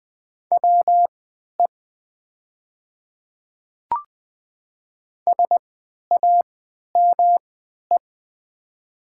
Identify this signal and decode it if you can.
Morse code, heard as a keyed tone, about 20 wpm, spaced as for 11 wpm, 700 Hz. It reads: WE SAME
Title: 3x Character Spacing - Top 300 Words - 20wpm